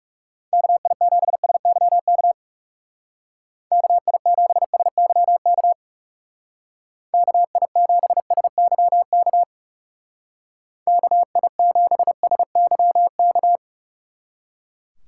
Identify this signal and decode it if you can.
Morse code, heard as a keyed tone, about 37 wpm, 700 Hz. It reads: KI7SYK KI7SYK KI7SYK KI7SYK
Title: Speed Racing - USA 2x3 Call Signs 30wpm